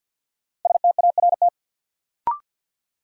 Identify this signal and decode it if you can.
Morse code, heard as a keyed tone, about 50 wpm, 700 Hz. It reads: START
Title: Sets of 4 Words - New Word Drill - 50wpm